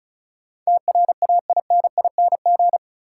Morse code, heard as a keyed tone, about 36 wpm, 700 Hz. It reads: TRAINING
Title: Sentences from Top 1400 Words - New Word Drill Alt - 35wpm